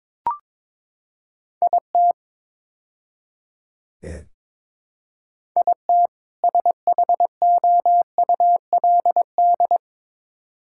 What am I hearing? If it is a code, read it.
Morse code, heard as a keyed tone, about 22 wpm, 700 Hz. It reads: IT IT SHOULD
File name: practice / 22wpm / Build A Phrase - 5-Word Phrases - 22wpm